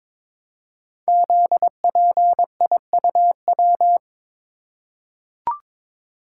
Morse code, heard as a keyed tone, about 22 wpm, 700 Hz. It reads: ZPIUW
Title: Code Groups - 5 Letters - 22wpm